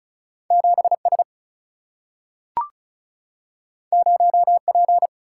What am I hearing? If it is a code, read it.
Morse code, heard as a keyed tone, about 36 wpm, 700 Hz. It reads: 7 S 0 P
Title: Single-Letter Single-Number 35wpm